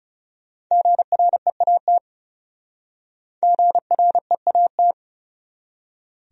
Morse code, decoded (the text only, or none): GREAT GREAT